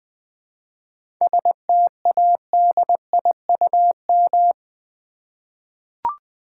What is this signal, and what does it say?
Morse code, heard as a keyed tone, about 20 wpm, 690 Hz. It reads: STADIUM
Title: 7 Letter Words - First 3 Letters Given - 20wpm